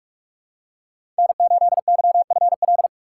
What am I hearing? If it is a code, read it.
Morse code, heard as a keyed tone, about 44 wpm, 700 Hz. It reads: N8YFL
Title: USA Call Signs - 5 Characters - 8x Character Spacing - 45wpm